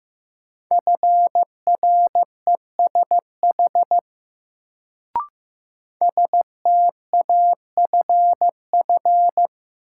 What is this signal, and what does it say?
Morse code, heard as a keyed tone, about 15 wpm, 700 Hz. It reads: FRESH STAFF
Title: Sentences from Top 1200 Words - New Word Drill Alt - 15wpm